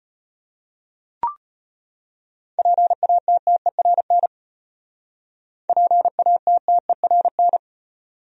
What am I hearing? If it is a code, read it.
Morse code, heard as a keyed tone, about 31 wpm, 700 Hz. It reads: PATTERN PATTERN
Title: Speed-Racing Top 500 Words 25wpm